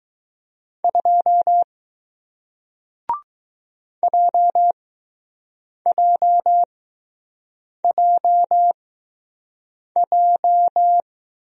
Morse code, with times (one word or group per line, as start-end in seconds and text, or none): 0.8-1.6 2
4.0-11.0 JJJJ